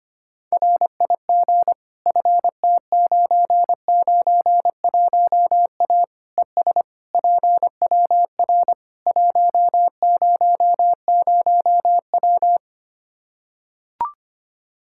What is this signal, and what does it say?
Morse code, heard as a keyed tone, about 25 wpm, 700 Hz. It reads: RIG FT991A ES PWR 100W
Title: Practice QSO - Exchange 2 - Info - 25wpm